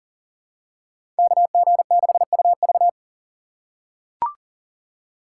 Morse code, heard as a keyed tone, about 40 wpm, 700 Hz. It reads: KG6UV
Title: USA Call Signs - 5 Characters - 3x Character Spacing - 40wpm